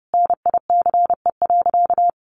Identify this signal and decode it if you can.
Morse code, heard as a keyed tone, about 30 wpm, 700 Hz. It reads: NICE.
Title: Easy Phrases 30wpm